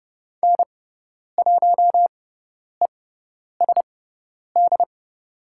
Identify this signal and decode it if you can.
Morse code, heard as a keyed tone, about 30 wpm, spaced as for 10 wpm, 700 Hz. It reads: N1ESD
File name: Lesson 16g - Letter c - Call Signs 30wpm